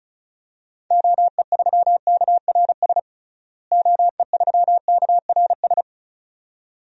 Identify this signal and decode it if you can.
Morse code, heard as a keyed tone, about 34 wpm, 700 Hz. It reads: OE3KRS OE3KRS
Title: DXCC-100 Call Signs - 6 Characters - Repeated 2x - 35wpm